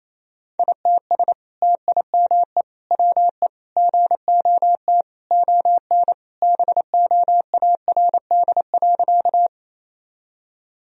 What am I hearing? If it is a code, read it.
Morse code, heard as a keyed tone, about 28 wpm, 700 Hz. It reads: ITS TIME WE GOT ON BOARD.
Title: Sentences from Top 1000 Words - Review Part 4 - 28wpm